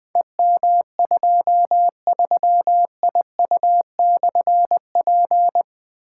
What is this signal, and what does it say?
Morse code, heard as a keyed tone, about 21 wpm, 700 Hz. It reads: EM23IU/P
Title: Difficult Call Signs 20wpm